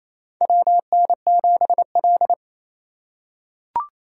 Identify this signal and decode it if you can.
Morse code, heard as a keyed tone, about 28 wpm, 700 Hz. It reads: WN7L
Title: USA Call Signs - 4 Characters - 4x Character Spacing - 28wpm